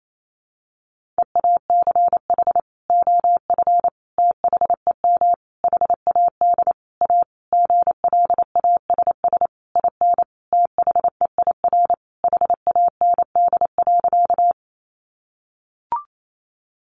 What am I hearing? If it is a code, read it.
Morse code, heard as a keyed tone, about 28 wpm, 700 Hz. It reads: EACH OF THEM HAD A GLASS IN THEIR HAND.
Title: Sentences from Top 1300 Words 28wpm